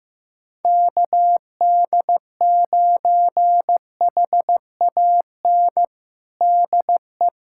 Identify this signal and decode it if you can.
Morse code, heard as a keyed tone, about 16 wpm, 700 Hz. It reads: KD9HAN DE
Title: USA Callsign DE Callsign 15wpm